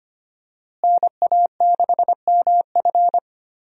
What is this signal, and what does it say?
Morse code, heard as a keyed tone, about 25 wpm, 700 Hz. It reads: NA6MF
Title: USA Call Signs - 5 Characters - Repeated 2x - 25wpm